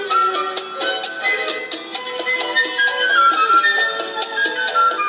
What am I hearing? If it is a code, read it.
Emergency Alert System (SAME) tones, heard as harsh buzzing bursts. It is heard from 1.1 to 4.7 s.